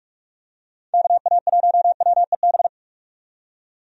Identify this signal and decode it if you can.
Morse code, heard as a keyed tone, about 43 wpm, 700 Hz. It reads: KA1WEB
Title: USA Call Signs - 6 Characters - 45wpm